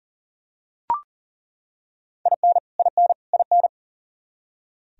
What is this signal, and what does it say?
Morse code, heard as a keyed tone, about 40 wpm, 700 Hz. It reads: IN IN IN
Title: Top 100 Words Repeated 40wpm